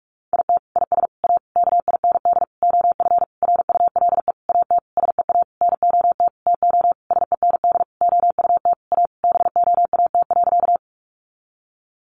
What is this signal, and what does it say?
Morse code, heard as a keyed tone, about 44 wpm, 700 Hz. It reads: IT IS A KIND OF RULE AT SEA NOT TO SEND OUT A BOAT.
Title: Sentences from Top 1500 Words 45wpm